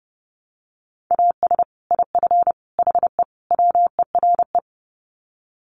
Morse code, heard as a keyed tone, about 30 wpm, 700 Hz. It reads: AS IF HE WERE
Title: Sets of 4 Words 30wpm